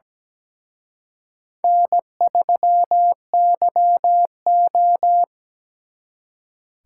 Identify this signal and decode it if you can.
Morse code, heard as a keyed tone, about 20 wpm, 700 Hz. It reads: N3YO
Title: Speed Racing - USA 1x2 Call Signs 17wpm